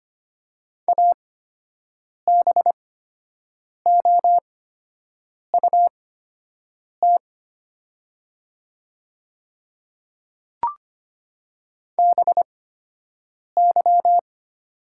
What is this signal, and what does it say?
Morse code, heard as a keyed tone, about 25 wpm, spaced as for 7 wpm, 700 Hz. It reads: ABOUT BY
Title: 8x Character Spacing - Top 100 Words - 25wpm